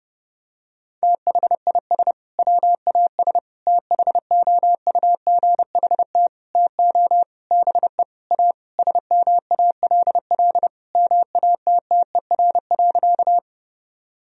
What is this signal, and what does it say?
Morse code, heard as a keyed tone, about 32 wpm, 700 Hz. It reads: THIS WAS THOUGHT TO BE A SMALL MATTER.
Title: Sentences from Top 1000 Words - Review Part 1 - 30wpm